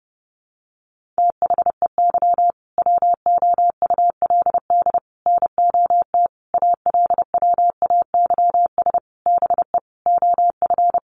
Morse code, heard as a keyed tone, about 30 wpm, 700 Hz. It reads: THEY WOULD NOT ALWAYS BE OF
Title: Sentences from Top 1400 Words 30wpm